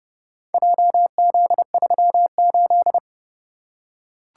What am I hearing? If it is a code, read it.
Morse code, heard as a keyed tone, about 27 wpm, 700 Hz. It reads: JZ38